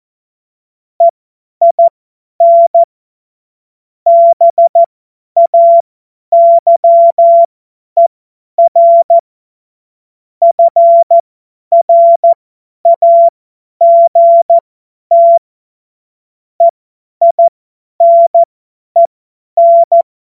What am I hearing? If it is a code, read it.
Morse code, heard as a keyed tone, about 14 wpm, spaced as for 10 wpm, 680 Hz. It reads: EIN BAYER FRAGT EINEN
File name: Witz-30-September-2025-50-F70